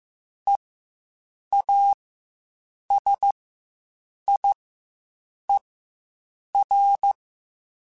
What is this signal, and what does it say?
Morse code, heard as a keyed tone, about 15 wpm, spaced as for 7 wpm, 800 Hz. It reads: EASIER
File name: Quotes Spaced / Quotes Spaced-mp3s / Character sp @10